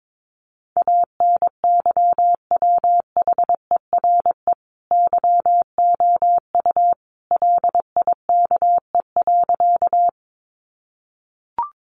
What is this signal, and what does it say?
Morse code, heard as a keyed tone, about 22 wpm, 700 Hz. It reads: ANYWHERE YOU LIKE.
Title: Sentences from Top 1400 Words 22wpm